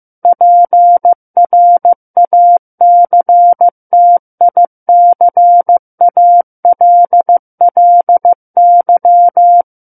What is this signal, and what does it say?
Morse code, heard as a keyed tone, about 15 wpm, 700 Hz. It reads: PRACTICALLY
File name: Book One / 06-Chapter 6 - 15 WPM